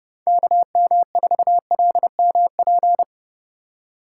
Morse code, heard as a keyed tone, about 30 wpm, 700 Hz. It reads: KM4LMP
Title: USA Call Signs - 6 Characters - 2x Character Spacing - 30wpm